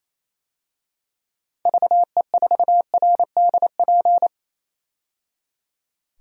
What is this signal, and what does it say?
Morse code, heard as a keyed tone, about 28 wpm, 700 Hz. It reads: VE4RDP
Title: DXCC-100 Call Signs - 6 Characters - 28wpm